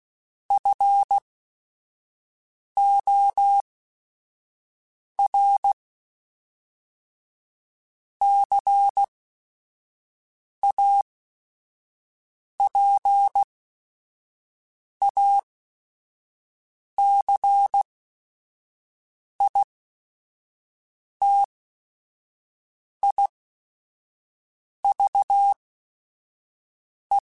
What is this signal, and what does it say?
Morse code, heard as a keyed tone, about 16 wpm, spaced as for 5 wpm, 800 Hz. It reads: FOR CAPACITIVE
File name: Technical Subjects / Technical Subjects-mp3s / XC @5